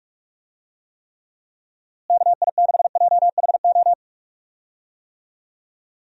Morse code, heard as a keyed tone, about 44 wpm, 700 Hz. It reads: KI6JHO
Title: USA Call Signs - 6 Characters - Repeated 2x - 45wpm